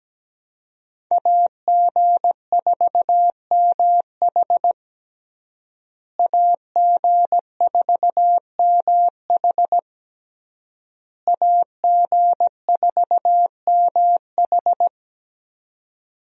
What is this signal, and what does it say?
Morse code, heard as a keyed tone, about 17 wpm, 700 Hz. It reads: AG4MH AG4MH AG4MH
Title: USA Call Signs - 5 Characters - Repeated 3x - 17wpm